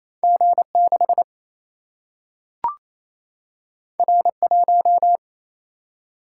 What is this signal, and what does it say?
Morse code, heard as a keyed tone, about 28 wpm, 700 Hz. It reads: G6 R1
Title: Single-Letter Single-Number 28wpm